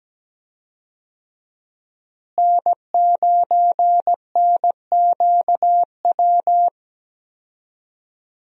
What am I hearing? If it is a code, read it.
Morse code, heard as a keyed tone, about 17 wpm, 700 Hz. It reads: N9NQW
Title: USA Call Signs - 5 Characters - Repeated 3x - 17wpm